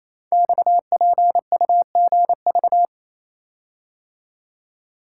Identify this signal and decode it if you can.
Morse code, heard as a keyed tone, about 27 wpm, 700 Hz. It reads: XPUGV
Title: Code Groups - 5 Letters - 28wpm